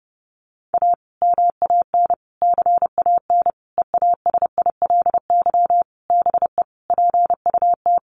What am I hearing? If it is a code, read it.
Morse code, heard as a keyed tone, about 30 wpm, 700 Hz. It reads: A MAN CAN EASILY BE PUT
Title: Sentences from Top 900 Words 30wpm